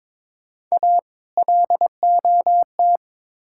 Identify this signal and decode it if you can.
Morse code, heard as a keyed tone, about 22 wpm, 700 Hz. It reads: A LOT
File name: practice / 22wpm / Build A Phrase - 5-Word Phrases - 22wpm